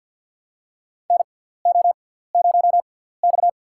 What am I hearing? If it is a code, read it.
Morse code, heard as a keyed tone, about 47 wpm, spaced as for 17 wpm, 700 Hz. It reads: NO0X
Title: USA Call Signs - 4 Characters - 6x Character Spacing - 50wpm